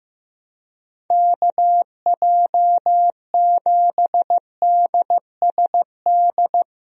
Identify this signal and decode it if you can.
Morse code, heard as a keyed tone, about 15 wpm, 700 Hz. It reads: KJ7DSD
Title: USA Call Signs - 6 Characters - Repeated 3x - 15wpm